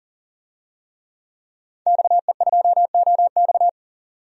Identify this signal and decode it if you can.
Morse code, heard as a keyed tone, about 37 wpm, 700 Hz. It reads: XE2OX
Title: DXCC-100 Call Signs - 5 Characters - 40wpm